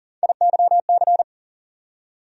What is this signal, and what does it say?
Morse code, heard as a keyed tone, about 40 wpm, 700 Hz. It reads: IYC
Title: USA Call Sign Postfixes - 3 Letters 40wpm